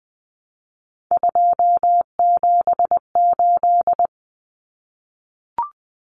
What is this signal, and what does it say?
Morse code, heard as a keyed tone, about 20 wpm, 700 Hz. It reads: 278
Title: Common Cut Numbers - 3-Digit 20wpm